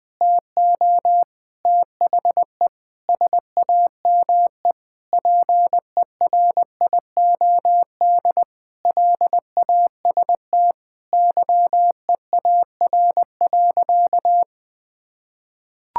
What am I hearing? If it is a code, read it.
Morse code, heard as a keyed tone, about 21 wpm, 700 Hz. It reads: TO THE SAME PERIOD LAST YEAR.
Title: Sentences with 7-Word Phrases - Spoken Beginning - 20wpm